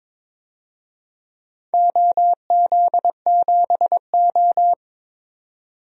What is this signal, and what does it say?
Morse code, heard as a keyed tone, about 22 wpm, 700 Hz. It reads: OZ7O
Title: DXCC-100 Call Signs - 4 Characters - 22wpm